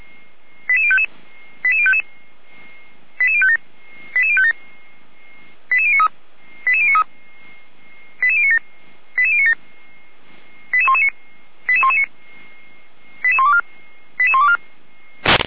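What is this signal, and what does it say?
Five-tone selective call (ZVEI), heard as quick runs of five tones, tones of about 70 ms. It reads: 88055 88055 88057 88057 88093 88093 88097 88097 88119 88119 88124 88124
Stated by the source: Received via SDR as a ZVEI five-tone call, address 88019, on the Feuerwehr Orlamünde fire brigade audio server.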